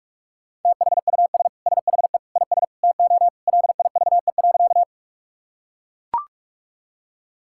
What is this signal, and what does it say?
Morse code, heard as a keyed tone, about 42 wpm, 700 Hz. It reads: THUS SHE IS TO LIVE.
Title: Sentences from Top 1000 Words - Review Part 5 - 45wpm